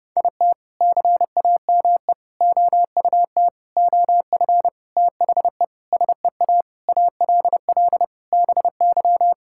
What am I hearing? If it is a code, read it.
Morse code, heard as a keyed tone, about 30 wpm, 700 Hz. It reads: IT CAME OUT OF THE SEA ALL BY
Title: Sentences from Top 1000 Words - Review Part 5 - 30wpm